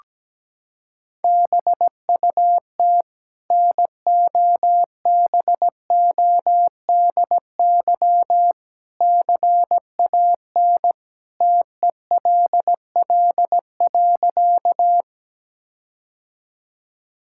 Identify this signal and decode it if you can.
Morse code, heard as a keyed tone, about 17 wpm, 700 Hz. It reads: BUT NOBODY CAN TELL.
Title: Sentences from Top 1600 Words 17wpm